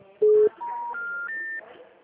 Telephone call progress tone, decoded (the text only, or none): special information tone